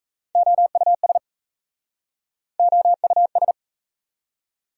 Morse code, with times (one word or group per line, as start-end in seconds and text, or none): 0.3-1.2 OUS
2.6-3.5 OUS